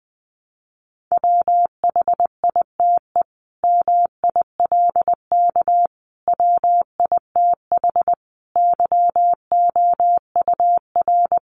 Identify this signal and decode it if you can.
Morse code, heard as a keyed tone, about 20 wpm, 700 Hz. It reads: WHITE MILK WITH YOUR
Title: Sentences from Top 1500 Words 20wpm